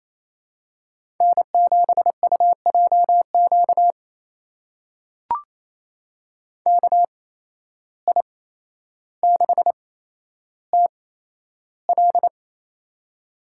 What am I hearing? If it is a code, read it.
Morse code, heard as a keyed tone, about 28 wpm, spaced as for 12 wpm, 700 Hz. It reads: N7UJQ KI6TL
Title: USA Call Signs - 5 Characters - 8x Character Spacing - 28wpm